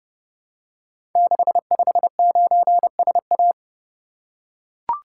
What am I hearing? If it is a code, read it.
Morse code, heard as a keyed tone, about 30 wpm, 700 Hz. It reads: 659SA
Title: Code Groups - 5 Letter:Numbers - 30wpm